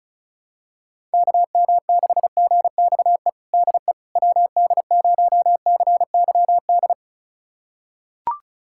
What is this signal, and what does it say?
Morse code, heard as a keyed tone, about 36 wpm, 700 Hz. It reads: KM6GXE DE WD0CYD
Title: USA Callsign DE Callsign - 6 Characters - 35wpm